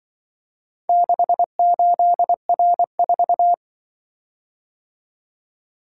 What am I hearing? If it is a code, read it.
Morse code, heard as a keyed tone, about 24 wpm, 700 Hz. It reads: 68R4
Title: Speed-Racing - Four Letter-Number Combinations - 20wpm